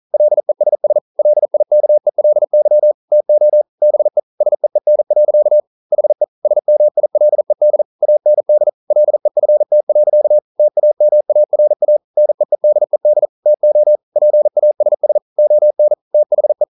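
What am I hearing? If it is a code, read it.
Morse code, heard as a keyed tone, about 42 wpm, 600 Hz. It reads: LESS LIKELY TO BE SEEN. HE SMILED AND LEFT. TAMARA NEEDED TO PASS ON THE
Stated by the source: Morse code practice at 41 wpm, from CW med Gnister.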